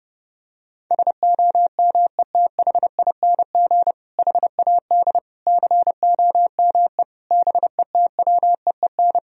Morse code, heard as a keyed tone, about 30 wpm, 700 Hz. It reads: SOMETHING HAD COME BETWEEN
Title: Sentences from Top 1000 Words - Review Part 1 - 30wpm